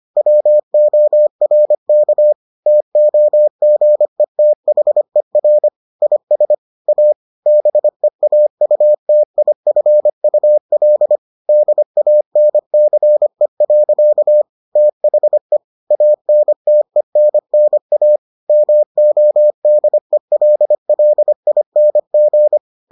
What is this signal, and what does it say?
Morse code, heard as a keyed tone, about 25 wpm, 600 Hz. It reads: WORK TOGETHER IS A BEAUTIFUL DANCE. THE ANTENNA MODELLING